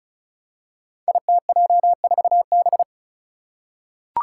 Morse code, heard as a keyed tone, about 36 wpm, 700 Hz. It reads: ITJ4B
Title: Five Challenging Letter-Number Combinations 35wpm